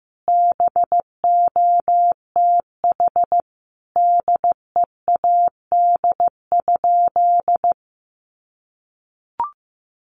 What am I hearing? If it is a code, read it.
Morse code, heard as a keyed tone, about 15 wpm, 700 Hz. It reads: BOTH DEAD?
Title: Sentences from Top 1500 Words 15wpm